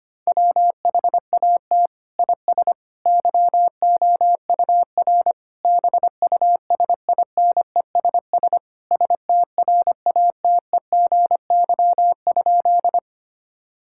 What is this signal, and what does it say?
Morse code, heard as a keyed tone, about 25 wpm, 700 Hz. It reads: WHAT IS YOUR BUSINESS STRATEGY?
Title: Sentences from Top 1000 Words - Review Part 5 - 25wpm